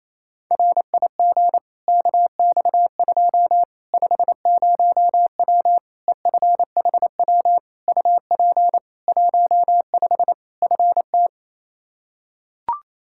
Morse code, heard as a keyed tone, about 28 wpm, 700 Hz. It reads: RIG KX2 50W EFHW UP 15 FT
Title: K1USN SST QSO Practice 28wpm